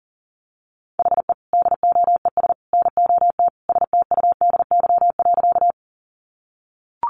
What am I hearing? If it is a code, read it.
Morse code, heard as a keyed tone, about 40 wpm, 700 Hz. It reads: HE DOES NOT STUDY.
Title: Additional Easy Phrases 40wpm